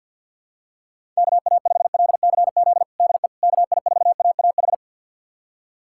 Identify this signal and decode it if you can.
Morse code, heard as a keyed tone, about 50 wpm, 700 Hz. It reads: KA5LCZ DE KI4AAH
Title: USA Callsign DE Callsign - 6 Characters - 50wpm